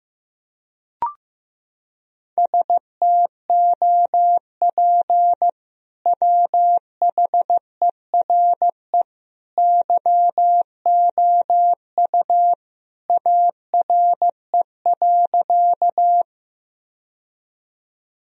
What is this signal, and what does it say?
Morse code, heard as a keyed tone, about 16 wpm, 700 Hz. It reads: STOP WHERE YOU ARE.
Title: Sentences from Top 1000 Words - Review Part 1 - 15wpm